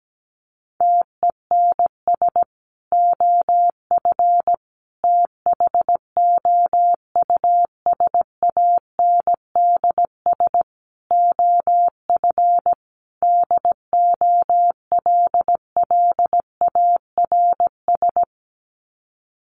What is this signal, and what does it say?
Morse code, heard as a keyed tone, about 17 wpm, 700 Hz. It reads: TENS OF THOUSANDS OF DOLLARS
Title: Sets of 5 Words Encore2 17wpm